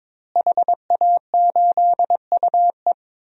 Morse code, heard as a keyed tone, about 22 wpm, 700 Hz. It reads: HA8UE
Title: DXCC-100 Call Signs - 5 Characters - Repeated 2x - 22wpm